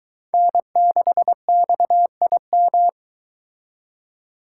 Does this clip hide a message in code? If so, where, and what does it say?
Yes: Morse code from 0.3 to 2.9 s, heard as a keyed tone, about 22 wpm, 700 Hz. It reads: N6XIM